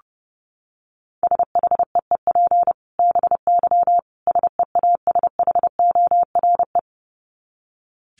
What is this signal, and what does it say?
Morse code, heard as a keyed tone, about 30 wpm, 700 Hz. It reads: SHEEP BY SEASHORE
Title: Dit Challenge 30wpm